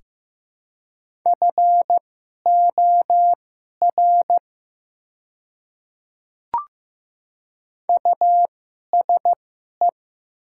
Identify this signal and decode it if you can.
Morse code, heard as a keyed tone, about 15 wpm, spaced as for 11 wpm, 700 Hz. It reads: FOR USE
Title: 2x Character Spacing - Top 200 Words - 15wpm